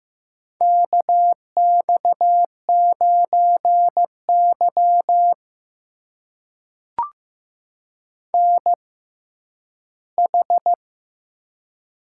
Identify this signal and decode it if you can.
Morse code, heard as a keyed tone, about 15 wpm, spaced as for 10 wpm, 700 Hz. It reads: KX9Y NH
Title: USA Call Signs - 4 Characters - 6x Character Spacing - 15wpm